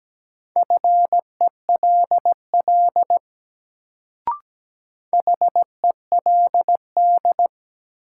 Morse code, heard as a keyed tone, about 17 wpm, 700 Hz. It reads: FELL HELD
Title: Sentences from Top 1600 Words - New Word Drill - 17wpm